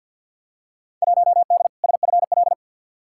Morse code, heard as a keyed tone, about 46 wpm, 700 Hz. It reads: 1D SFL
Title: Field Day Exchanges - 50wpm